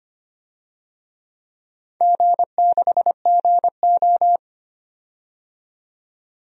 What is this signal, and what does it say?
Morse code, heard as a keyed tone, about 25 wpm, 700 Hz. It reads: G6GO
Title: DXCC-100 Call Signs - 4 Characters - 25wpm